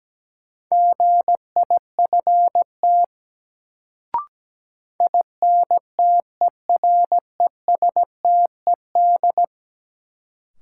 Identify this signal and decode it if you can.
Morse code, heard as a keyed tone, about 17 wpm, 700 Hz. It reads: GIFT INTERESTED
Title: Sentences from Top 900 Words - New Word Drill - 17wpm